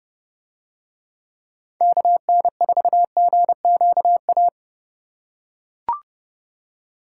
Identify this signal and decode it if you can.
Morse code, heard as a keyed tone, about 27 wpm, 700 Hz. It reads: KN4GQA